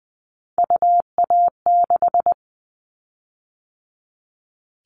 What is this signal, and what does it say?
Morse code, heard as a keyed tone, about 20 wpm, 700 Hz. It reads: UA6
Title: Callsign Prefixes for Top 100 DXCC - 20wpm